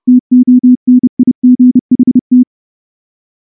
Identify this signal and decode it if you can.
Morse code, heard as a keyed tone, about 29 wpm, 250 Hz. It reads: TONIGHT
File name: morse / morse-training-30WPM